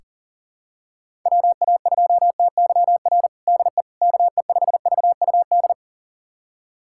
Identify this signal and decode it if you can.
Morse code, heard as a keyed tone, about 40 wpm, 700 Hz. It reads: WA2TYR DE KE5VUD
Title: USA Callsign DE Callsign - 6 Characters - 40wpm